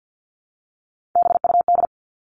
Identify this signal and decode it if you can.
Morse code, heard as a keyed tone, about 50 wpm, 700 Hz. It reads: BUD
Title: All Cut Numbers - 3-Digit 50wpm